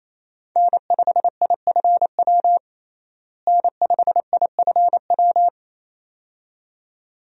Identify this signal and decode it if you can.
Morse code, heard as a keyed tone, about 28 wpm, 700 Hz. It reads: N5IFW N5IFW
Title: USA Call Signs - 5 Characters - Repeated 2x - 28wpm